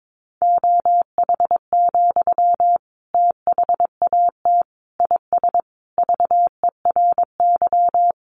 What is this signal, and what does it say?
Morse code, heard as a keyed tone, about 22 wpm, 700 Hz. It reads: OH, THAT IS VERY
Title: Sentences from Top 500 Words 22wpm